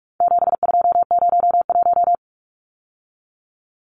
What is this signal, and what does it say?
Morse code, heard as a keyed tone, about 44 wpm, 700 Hz. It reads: 7201
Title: Numbers - 4-Digit 45wpm